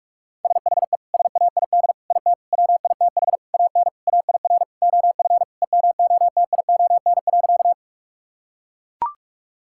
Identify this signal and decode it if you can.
Morse code, heard as a keyed tone, about 42 wpm, 700 Hz. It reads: SHE SAID IT WITH AN AIR OF EMOTION.
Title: Sentences from Top 1600 Words 45wpm